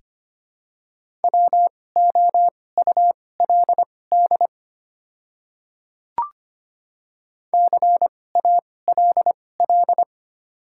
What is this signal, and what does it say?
Morse code, heard as a keyed tone, about 25 wpm, spaced as for 18 wpm, 700 Hz. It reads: WOULD CALL
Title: 2x Character Spacing - Top 100 Words - 25wpm